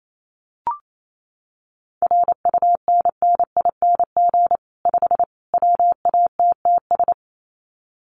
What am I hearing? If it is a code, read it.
Morse code, heard as a keyed tone, about 28 wpm, 700 Hz. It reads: RUNNING 5 WATTS
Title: Common QSO Phrases 28wpm